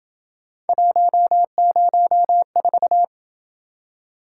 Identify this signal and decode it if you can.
Morse code, heard as a keyed tone, about 28 wpm, 700 Hz. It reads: 104